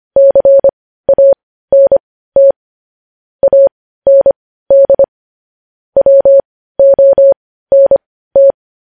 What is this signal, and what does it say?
Morse code, heard as a keyed tone, about 25 wpm, spaced as for 15 wpm, 550 Hz. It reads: CANT AND WONT
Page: News Headlines in Morse Code at 15 WPM